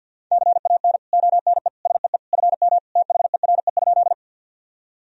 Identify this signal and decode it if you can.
Morse code, heard as a keyed tone, about 46 wpm, 700 Hz. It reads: CAN ONE SEE FM THERE?
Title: Sentences from Top 100 Words and Abbreviations 50wpm